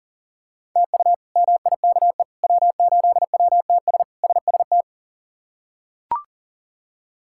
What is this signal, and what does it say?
Morse code, heard as a keyed tone, about 40 wpm, 700 Hz. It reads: TU MIKE W8WTS SST
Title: K1USN SST QSO Practice - 40wpm